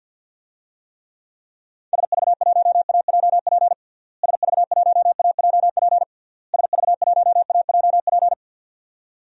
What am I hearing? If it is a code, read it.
Morse code, heard as a keyed tone, about 52 wpm, 700 Hz. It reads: SV1AJP SV1AJP SV1AJP
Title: DXCC-100 Call Signs - 6 Characters - Repeated 3x - 50wpm